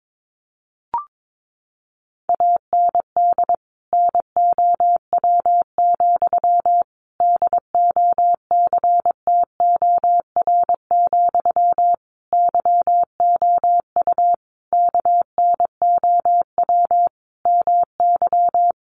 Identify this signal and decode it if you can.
Morse code, heard as a keyed tone, about 22 wpm, 700 Hz. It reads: AND NOW, DOCTOR, YOU KNOW MY
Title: Sentences from Top 1500 Words 22wpm